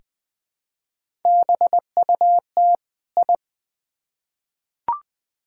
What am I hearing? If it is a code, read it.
Morse code, heard as a keyed tone, about 21 wpm, 700 Hz. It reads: BUT I
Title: Build A Phrase - 2-Word Phrases - 20wpm